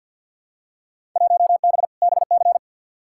Morse code, heard as a keyed tone, about 53 wpm, 700 Hz. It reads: 1B BC
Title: Field Day Exchanges - 50wpm